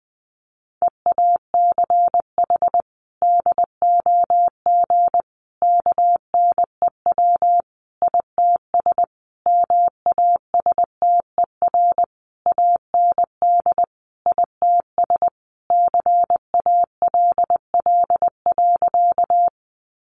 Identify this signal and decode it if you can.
Morse code, heard as a keyed tone, about 20 wpm, 700 Hz. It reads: EACH DOG KNEW ITS MASTER AND ITS CALL.
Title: Sentences from Top 1300 Words 20wpm